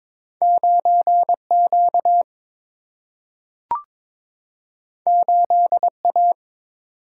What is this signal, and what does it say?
Morse code, heard as a keyed tone, about 22 wpm, 700 Hz. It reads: 9 Q 8 A
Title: Single-Letter Single-Number 22wpm